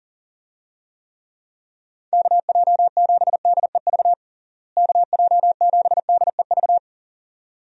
Morse code, heard as a keyed tone, about 40 wpm, 700 Hz. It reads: KJ7DEV KJ7DEV
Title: USA Call Signs - 6 Characters - Repeated 2x - 40wpm